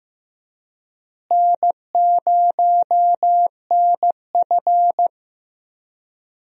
Morse code, heard as a keyed tone, about 15 wpm, 700 Hz. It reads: N0NF
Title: USA Call Signs - 4 Characters - Repeated 3x - 15wpm